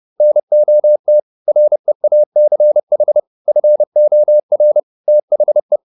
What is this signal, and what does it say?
Morse code, heard as a keyed tone, about 30 wpm, 600 Hz. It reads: NOT REACH FOR THE